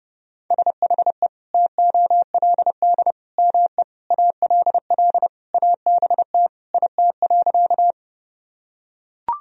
Morse code, heard as a keyed tone, about 30 wpm, 700 Hz. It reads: SHE TOLD ME ALL ABT IT.
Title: Sentences from Top 400 Words and Abbreviations 30wpm